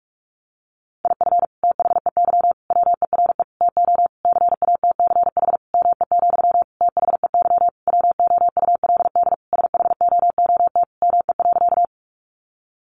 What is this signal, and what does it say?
Morse code, heard as a keyed tone, about 44 wpm, 700 Hz. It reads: IF THEY WERE TO CATCH ME, THEY WOULD SHOOT ME.
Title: Sentences from Top 1500 Words 45wpm